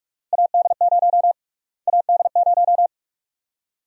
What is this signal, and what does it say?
Morse code, heard as a keyed tone, about 43 wpm, 700 Hz. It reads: AD0 AD0
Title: USA Call Sign Prefixes - Repeated 2x - 45wpm